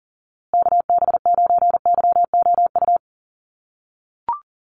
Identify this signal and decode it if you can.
Morse code, heard as a keyed tone, about 40 wpm, 700 Hz. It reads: KB9YOU
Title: USA Call Signs Repeated 2x 40wpm